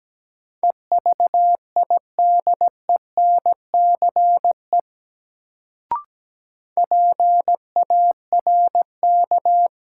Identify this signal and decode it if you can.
Morse code, heard as a keyed tone, about 17 wpm, 700 Hz. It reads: EVIDENCE PARK
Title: Sentences from Top 1300 Words - New Word Drill - 17wpm